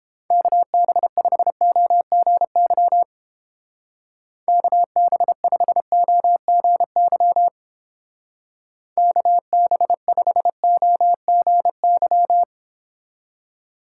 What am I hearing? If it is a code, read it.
Morse code, heard as a keyed tone, about 30 wpm, 700 Hz. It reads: KB5OGY KB5OGY KB5OGY